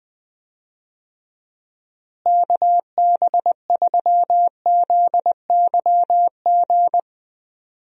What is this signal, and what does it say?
Morse code, heard as a keyed tone, about 20 wpm, 700 Hz. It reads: KB3ZYG
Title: USA Call Signs - 6 Characters - Repeated 3x - 20wpm